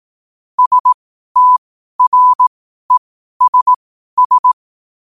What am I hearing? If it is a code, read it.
Morse code, heard as a keyed tone, about 18 wpm, spaced as for 12 wpm, 1000 Hz. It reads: STRESS
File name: qotdfiles / 12wpm